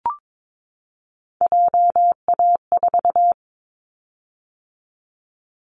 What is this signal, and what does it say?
Morse code, heard as a keyed tone, about 22 wpm, 700 Hz. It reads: JA4
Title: Callsign Prefixes for Top 50 DXCC - 22wpm